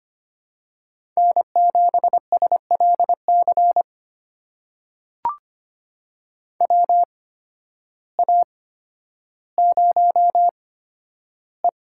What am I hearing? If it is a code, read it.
Morse code, heard as a keyed tone, about 25 wpm, spaced as for 12 wpm, 700 Hz. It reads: N7SLC WA0E
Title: USA Call Signs - 8x Character Spacing - 25wpm